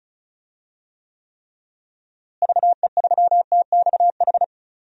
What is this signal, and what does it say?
Morse code, heard as a keyed tone, about 36 wpm, 700 Hz. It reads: VE3TXH
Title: DXCC-100 Call Signs - 6 Characters - Repeated 3x - 35wpm